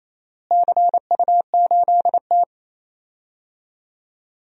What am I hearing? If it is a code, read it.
Morse code, heard as a keyed tone, about 28 wpm, 700 Hz. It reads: CU8T
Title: DXCC-100 Call Signs - 4 Characters - Repeated 2x - 28wpm